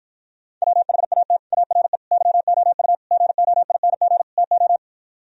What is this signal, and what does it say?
Morse code, heard as a keyed tone, about 52 wpm, 700 Hz. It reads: WHAT ARE YOU GOING TO